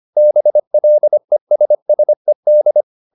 Morse code, heard as a keyed tone, about 25 wpm, 600 Hz. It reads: BLESSED
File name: the old man and the sea part 1 25wpm